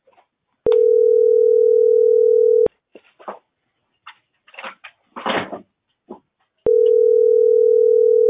Ringback tone (North America), 440 + 480 Hz.